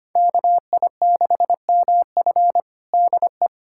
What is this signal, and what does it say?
Morse code, heard as a keyed tone, about 25 wpm, 700 Hz. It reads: KI6MF DE
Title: USA Callsign DE Callsign - 5 Characters - 25wpm